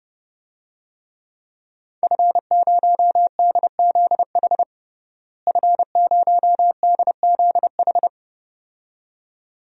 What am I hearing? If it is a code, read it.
Morse code, heard as a keyed tone, about 30 wpm, 700 Hz. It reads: F0DZH F0DZH
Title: DXCC-100 Call Signs - 5 Characters - Repeated 2x - 30wpm